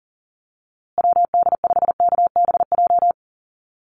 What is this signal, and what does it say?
Morse code, heard as a keyed tone, about 40 wpm, 700 Hz. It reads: WD5KBJ
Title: USA Call Signs Repeated 2x 40wpm